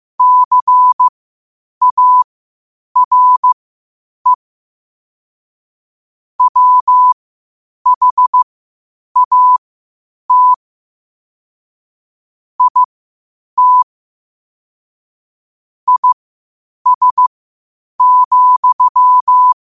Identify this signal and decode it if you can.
Morse code, heard as a keyed tone, about 15 wpm, spaced as for 8 wpm, 1000 Hz. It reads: CARE WHAT IT IS,
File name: qotdfiles / 8wpm